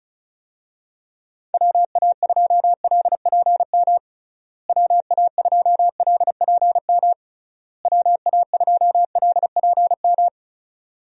Morse code, heard as a keyed tone, about 36 wpm, 700 Hz. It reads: WA2LPM WA2LPM WA2LPM
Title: USA Call Signs - 6 Characters - Repeated 3x - 35wpm